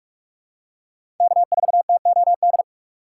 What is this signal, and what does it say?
Morse code, heard as a keyed tone, about 46 wpm, 700 Hz. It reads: K4TOD
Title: USA Call Signs - 5 Characters - 6x Character Spacing - 45wpm